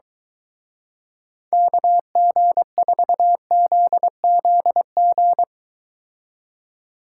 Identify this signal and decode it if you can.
Morse code, heard as a keyed tone, about 23 wpm, 700 Hz. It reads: KG4ZZG